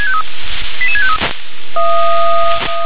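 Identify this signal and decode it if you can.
Five-tone selective call (ZVEI), heard as quick runs of five tones, tones of about 70 ms. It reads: 88653
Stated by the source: Five-tone ZVEI selective call, received via SDR.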